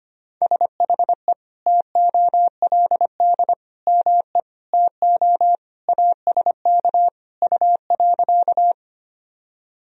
Morse code, heard as a keyed tone, about 25 wpm, 700 Hz. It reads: SHE TOLD ME TO ASK U.
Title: Sentences from Top 400 Words and Abbreviations 25wpm